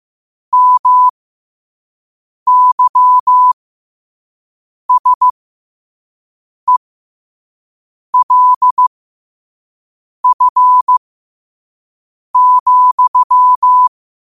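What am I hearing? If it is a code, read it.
Morse code, heard as a keyed tone, about 15 wpm, spaced as for 5 wpm, 1000 Hz. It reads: MYSELF,
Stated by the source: Morse quote of the day at 5 wpm.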